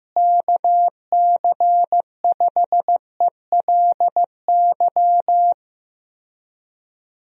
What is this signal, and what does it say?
Morse code, heard as a keyed tone, about 15 wpm, 700 Hz. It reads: KC5ELY
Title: USA Call Signs - 6 Characters - Repeated 2x - 15wpm